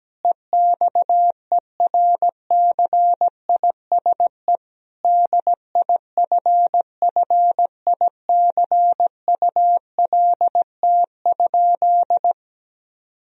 Morse code, heard as a keyed tone, about 17 wpm, 700 Hz. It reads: EXERCISE DIFFICULT?
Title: Sentences from Top 1000 Words - Review Part 4 - 17wpm